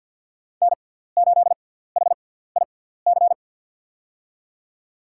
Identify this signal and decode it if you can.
Morse code, heard as a keyed tone, about 53 wpm, spaced as for 17 wpm, 700 Hz. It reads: N8HIC
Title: USA Call Signs - 6x Character Spacing - 50wpm